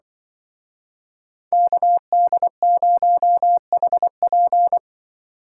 Morse code, heard as a keyed tone, about 25 wpm, 700 Hz. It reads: KD0HP
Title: Speed Racing - USA Call Signs 20wpm